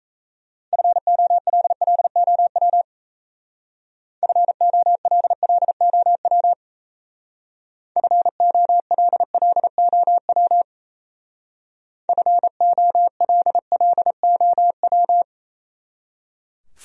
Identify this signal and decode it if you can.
Morse code, heard as a keyed tone, about 36 wpm, 700 Hz. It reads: FOLLOW FOLLOW FOLLOW FOLLOW